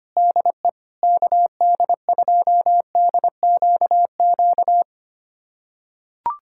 Morse code, heard as a keyed tone, about 25 wpm, 700 Hz. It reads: DE KD2DQQ
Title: USA Callsign DE Callsign - 6 Characters - 25wpm